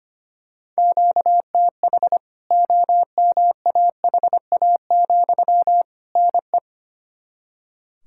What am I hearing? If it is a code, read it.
Morse code, heard as a keyed tone, about 25 wpm, 700 Hz. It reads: QTH OMAHA, NE
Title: K1USN SST QSO Practice 25wpm